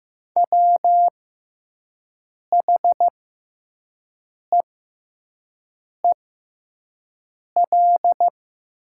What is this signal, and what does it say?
Morse code, heard as a keyed tone, about 15 wpm, spaced as for 5 wpm, 700 Hz. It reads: WHEEL
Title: 6x Character Spacing - Top 500 Words - 15wpm